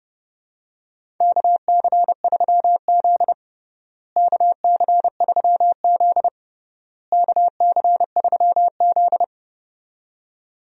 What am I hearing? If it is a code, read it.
Morse code, heard as a keyed tone, about 32 wpm, 700 Hz. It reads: KC3Z KC3Z KC3Z
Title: USA Call Signs - 4 Characters - Repeated 3x - 30wpm